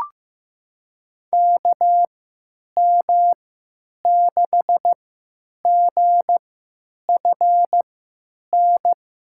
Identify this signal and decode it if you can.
Morse code, heard as a keyed tone, about 16 wpm, spaced as for 9 wpm, 700 Hz. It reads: KM6GFN
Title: USA Call Signs - 6 Characters - 3x Character Spacing - 15wpm